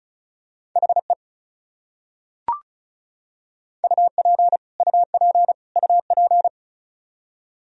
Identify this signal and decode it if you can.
Morse code, heard as a keyed tone, about 34 wpm, 700 Hz. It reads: HE UP UP UP
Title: Top 100 Words Repeated 35wpm